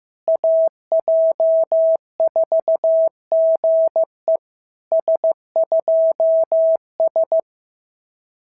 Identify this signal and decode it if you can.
Morse code, heard as a keyed tone, about 15 wpm, 650 Hz. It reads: AJ4GE S2S
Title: SOTA QSO Practice 15wpm